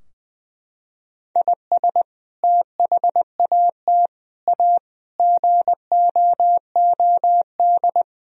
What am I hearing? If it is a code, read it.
Morse code, heard as a keyed tone, about 20 wpm, 700 Hz. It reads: IS THAT A GOOD
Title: Sentences from Top 1000 Words - Review Part 3 - 20wpm